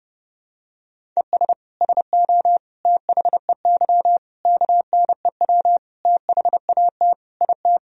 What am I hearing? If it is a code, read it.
Morse code, heard as a keyed tone, about 30 wpm, 700 Hz. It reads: ES SO THEY KNEW THAT IT
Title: Sentences from Top 400 Words and Abbreviations 30wpm